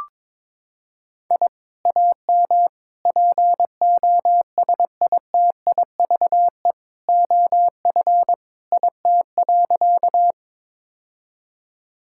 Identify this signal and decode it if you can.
Morse code, heard as a keyed tone, about 22 wpm, 700 Hz. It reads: I AM POSITIVE OF IT.
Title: Sentences from Top 1000 Words - Review Part 5 - 22wpm